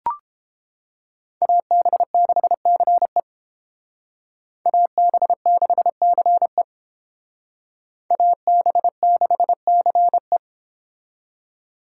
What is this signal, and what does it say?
Morse code, heard as a keyed tone, about 32 wpm, 700 Hz. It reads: AB6CE AB6CE AB6CE